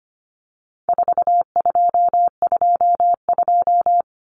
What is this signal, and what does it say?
Morse code, heard as a keyed tone, about 25 wpm, 700 Hz. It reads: 4222
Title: Common Cut Numbers - 4-Digit 25wpm